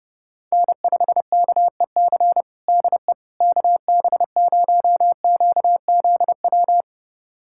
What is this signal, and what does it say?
Morse code, heard as a keyed tone, about 32 wpm, 700 Hz. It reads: N5KEC DE KB0QZW
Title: USA Callsign DE Callsign 30wpm